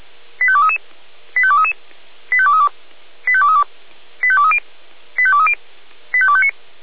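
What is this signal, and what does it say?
Five-tone selective call (ZVEI), heard as quick runs of five tones, tones of about 70 ms. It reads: 85233 85233 85232 85232 85230 85230 85259
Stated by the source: ZVEI five-tone selective call for 85852, received by SDR, from Feuerwehr Orlamünde.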